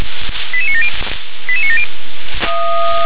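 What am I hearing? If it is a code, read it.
Five-tone selective call (ZVEI), heard as quick runs of five tones, tones of about 70 ms. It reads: 88970 88970
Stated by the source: Five-tone ZVEI selective call received on an SDR.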